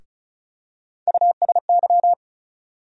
Morse code, heard as a keyed tone, about 36 wpm, 700 Hz. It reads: USY
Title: USA Call Sign Postfixes - 3 Letters 35wpm